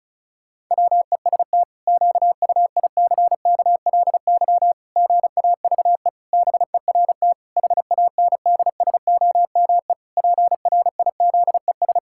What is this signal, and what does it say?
Morse code, heard as a keyed tone, about 34 wpm, 700 Hz. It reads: WEST QUICKLY GAVE BERT HANDSOME PRIZES
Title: Pangram Challenge v2 - 35wpm